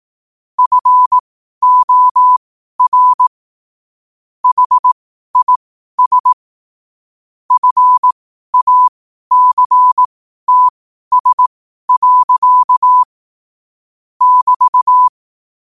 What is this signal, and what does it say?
Morse code, heard as a keyed tone, about 18 wpm, spaced as for 12 wpm, 1000 Hz. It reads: FOR HIS FACTS. <BT>
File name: qotdfiles / 12wpm